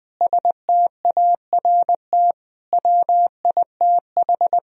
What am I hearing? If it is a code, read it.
Morse code, heard as a keyed tone, about 20 wpm, 700 Hz. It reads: START WITH
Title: Sentences from Top 1000 Words - Review Part 1 - 20wpm